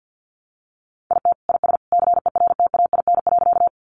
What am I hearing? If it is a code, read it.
Morse code, heard as a keyed tone, about 50 wpm, 700 Hz. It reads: IT IS CERTAIN.
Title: Sentences from Top 500 Words 50wpm